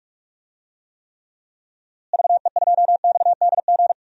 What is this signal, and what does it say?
Morse code, heard as a keyed tone, about 46 wpm, 700 Hz. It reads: VE2XDG
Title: DXCC-100 Call Signs - 6 Characters - Repeated 3x - 45wpm